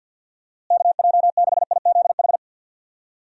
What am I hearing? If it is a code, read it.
Morse code, heard as a keyed tone, about 51 wpm, 700 Hz. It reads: KJ6IZH
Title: USA Call Signs - 6 Characters - Repeated 2x - 50wpm